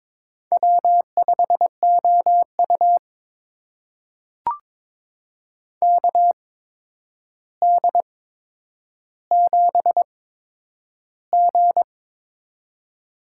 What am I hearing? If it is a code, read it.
Morse code, heard as a keyed tone, about 22 wpm, spaced as for 9 wpm, 700 Hz. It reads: W5OU KD7G
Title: USA Call Signs - 4 Characters - 8x Character Spacing - 22wpm